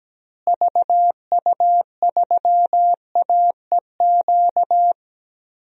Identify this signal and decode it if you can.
Morse code, heard as a keyed tone, about 17 wpm, 700 Hz. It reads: VU3AEQ
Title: DXCC-100 Call Signs - 6 Characters - Repeated 3x - 17wpm